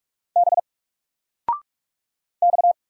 Morse code, heard as a keyed tone, about 44 wpm, 700 Hz. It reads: B X
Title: Single Letter-Number 45wpm